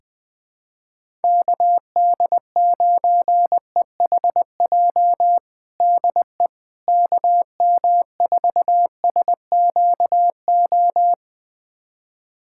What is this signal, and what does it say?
Morse code, heard as a keyed tone, about 21 wpm, 700 Hz. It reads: KD9EHJ DE KM4SQO
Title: USA Callsign DE Callsign 20wpm